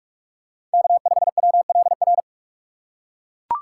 Morse code, heard as a keyed tone, about 44 wpm, 690 Hz. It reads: K5WLR
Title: USA Call Signs - 5 Characters - 2x Character Spacing - 45wpm